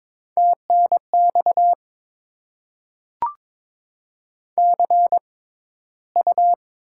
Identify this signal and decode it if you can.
Morse code, heard as a keyed tone, about 22 wpm, spaced as for 13 wpm, 690 Hz. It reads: TNX CU
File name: Top 20 QSO Elements - 6x Character Spacing - 22wpm